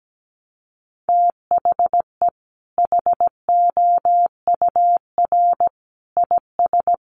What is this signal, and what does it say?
Morse code, heard as a keyed tone, about 17 wpm, 700 Hz. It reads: THE HOUR IS
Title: Sentences from Top 1100 Words 17wpm